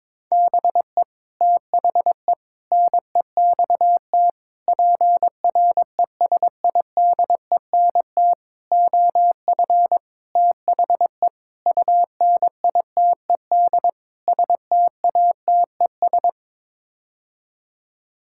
Morse code, heard as a keyed tone, about 22 wpm, 700 Hz. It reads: BE THE NEXT PRESIDENT OF THE UNITED STATES
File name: practice / 22wpm / Sets of 8 Words - 22wpm